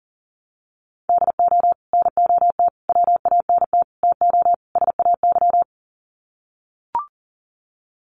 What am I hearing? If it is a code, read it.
Morse code, heard as a keyed tone, about 40 wpm, 700 Hz. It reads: DO NOT WANT TO SAY
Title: Sets of 5 Words Encore 40wpm